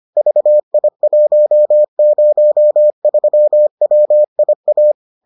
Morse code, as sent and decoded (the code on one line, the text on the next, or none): ...- .. .---- ----- ...-- .-- .. .-
VI103WIA